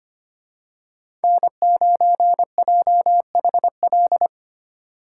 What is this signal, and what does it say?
Morse code, heard as a keyed tone, about 25 wpm, 700 Hz. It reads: N9JHL
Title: K1USN SST QSO Practice - 25wpm